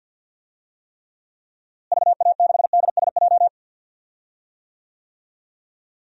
Morse code, heard as a keyed tone, about 51 wpm, 700 Hz. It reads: VA6DSJ
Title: DXCC-100 Call Signs - 6 Characters - Repeated 2x - 50wpm